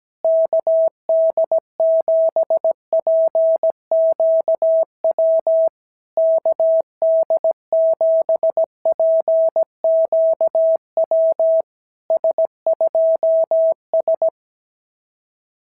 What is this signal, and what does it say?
Morse code, heard as a keyed tone, about 17 wpm, 650 Hz. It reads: KD7PQW KD7PQW S2S
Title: SOTA QSO Practice 17wpm